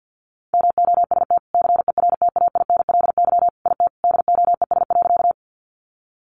Morse code, heard as a keyed tone, about 50 wpm, 700 Hz. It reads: MOST CERTAINLY IT DOES.
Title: Sentences from Top 900 Words 50wpm